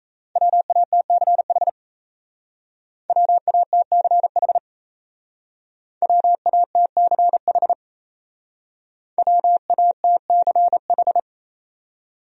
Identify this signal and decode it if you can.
Morse code, heard as a keyed tone, about 38 wpm, 700 Hz. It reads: WATCH WATCH WATCH WATCH